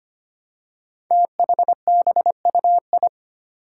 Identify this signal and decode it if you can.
Morse code, heard as a keyed tone, about 25 wpm, 700 Hz. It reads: THBUI
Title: Five Challenging Letter-Number Combinations 25wpm